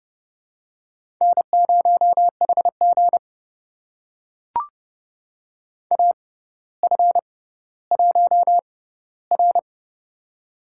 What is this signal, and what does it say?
Morse code, heard as a keyed tone, about 30 wpm, spaced as for 15 wpm, 700 Hz. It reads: N0HG AF1R
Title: USA Call Signs - 4 Characters - 6x Character Spacing - 30wpm